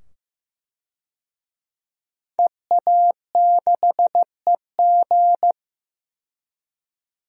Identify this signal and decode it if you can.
Morse code, heard as a keyed tone, about 15 wpm, 700 Hz. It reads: EA6EG
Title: DXCC-100 Call Signs - 5 Characters - 15wpm